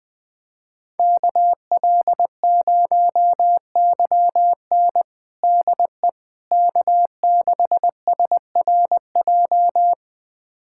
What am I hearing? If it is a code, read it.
Morse code, heard as a keyed tone, about 20 wpm, 700 Hz. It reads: KL0YN DE K6SRJ
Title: USA Callsign DE Callsign - 5 Characters - 20wpm